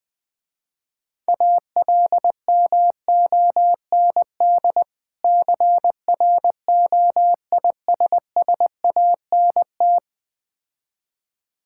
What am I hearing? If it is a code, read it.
Morse code, heard as a keyed tone, about 20 wpm, 700 Hz. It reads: ALMOND CROISSANT
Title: Difficult Words with Hints - 20wpm